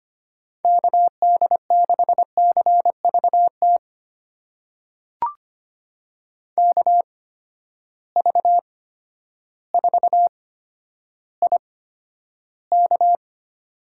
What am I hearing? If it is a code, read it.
Morse code, heard as a keyed tone, about 25 wpm, spaced as for 11 wpm, 700 Hz. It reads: KD6CVT KV4IK
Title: USA Call Signs - 8x Character Spacing - 25wpm